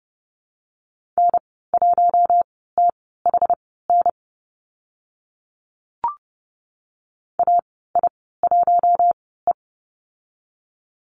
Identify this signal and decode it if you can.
Morse code, heard as a keyed tone, about 30 wpm, spaced as for 17 wpm, 700 Hz. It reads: N1THN AI1E
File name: Lesson 14g - Number 2 - Call Signs 30wpm